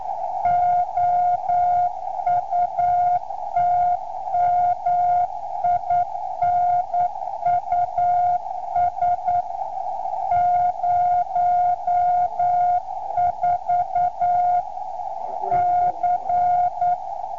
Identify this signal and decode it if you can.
Morse code, heard as a keyed tone, about 9 wpm, 710 Hz. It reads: OUT MINUS 04 C